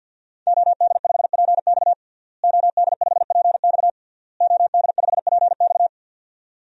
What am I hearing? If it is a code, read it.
Morse code, heard as a keyed tone, about 52 wpm, 700 Hz. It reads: OD5PX OD5PX OD5PX
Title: DXCC-100 Call Signs - 5 Characters - Repeated 3x - 50wpm